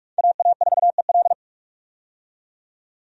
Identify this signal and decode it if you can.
Morse code, heard as a keyed tone, about 47 wpm, 700 Hz. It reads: AA4EL
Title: USA Call Signs - 5 Characters - 45wpm